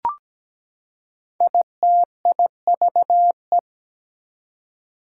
Morse code, heard as a keyed tone, about 17 wpm, 700 Hz. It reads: ITIVE
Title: Prefixes and Suffixes 17wpm